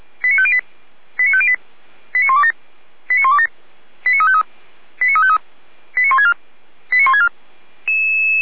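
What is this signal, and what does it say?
Five-tone selective call (ZVEI), heard as quick runs of five tones, tones of about 70 ms. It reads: 89508 89508 89127 89127 89353 89353 89164 89164
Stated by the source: ZVEI five-tone call 89235 received by SDR, archived by Feuerwehr Orlamünde.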